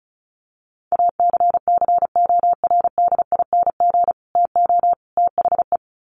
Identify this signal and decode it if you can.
Morse code, heard as a keyed tone, about 34 wpm, 700 Hz. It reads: ACCORDING TO THE
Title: Sets of 3 Words 35wpm